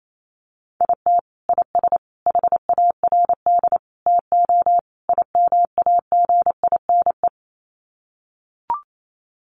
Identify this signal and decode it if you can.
Morse code, heard as a keyed tone, about 28 wpm, 700 Hz. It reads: IT IS HARD TO IMAGINE
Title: Sets of 5 Words Encore2 28wpm